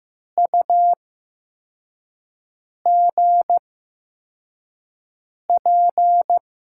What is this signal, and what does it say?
Morse code, heard as a keyed tone, about 15 wpm, spaced as for 4 wpm, 700 Hz. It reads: UGP
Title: USA Call Signs - 6 Characters - 8x Character Spacing - 15wpm